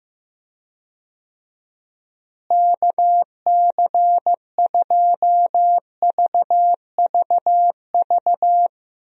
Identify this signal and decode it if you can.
Morse code, heard as a keyed tone, about 15 wpm, 700 Hz. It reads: KC2VVV
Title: USA Call Signs - 6 Characters - Repeated 2x - 15wpm